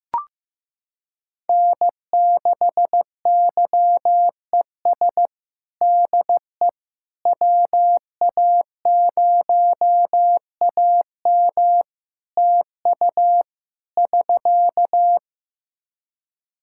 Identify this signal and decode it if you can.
Morse code, heard as a keyed tone, about 15 wpm, 700 Hz. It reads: N6YES DE WA0AM TU <SK>
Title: Practice QSO - Exchange 4 - Ending - 15wpm